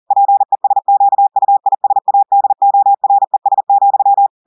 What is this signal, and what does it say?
Morse code, heard as a keyed tone, about 40 wpm, 800 Hz. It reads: PESQUISADORES,
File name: audios / portugais / portugais texte en clair 1 vitesse 40 mots minutes